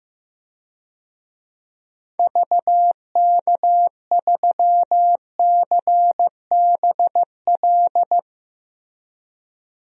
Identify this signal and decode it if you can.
Morse code, heard as a keyed tone, about 16 wpm, 700 Hz. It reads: VK3CBL
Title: DXCC-100 Call Signs - 6 Characters - Repeated 2x - 15wpm